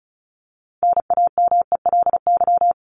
Morse code, heard as a keyed tone, about 34 wpm, 700 Hz. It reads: NAMELY
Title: LY Suffix - Top 100 Words with Base Words 35wpm